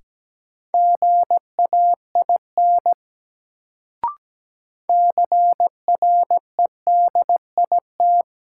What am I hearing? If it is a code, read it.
Morse code, heard as a keyed tone, about 17 wpm, 700 Hz. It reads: GAIN CREDIT
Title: Sentences from Top 1000 Words - New Word Drill - 17wpm